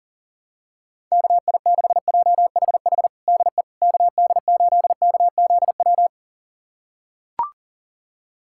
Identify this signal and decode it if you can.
Morse code, heard as a keyed tone, about 40 wpm, 700 Hz. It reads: KI6JHH DE KD8KZW
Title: USA Callsign DE Callsign 40wpm